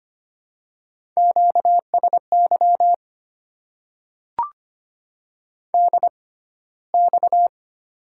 Morse code, heard as a keyed tone, about 25 wpm, spaced as for 15 wpm, 700 Hz. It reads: QSY DX
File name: Top 90 QSO Elements - 6x Character Spacing - 25wpm